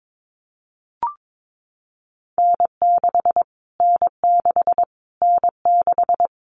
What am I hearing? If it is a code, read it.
Morse code, heard as a keyed tone, about 22 wpm, 700 Hz. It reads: N6 N6 N6
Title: Callsign Prefixes for Top 25 DXCC - Repeated 3x - 22wpm